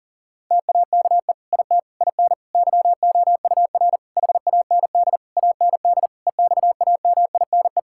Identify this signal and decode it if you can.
Morse code, heard as a keyed tone, about 37 wpm, 700 Hz. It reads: TAKE IT IN YOUR HAND AND EXAMINE
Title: Sentences from Top 1600 Words 40wpm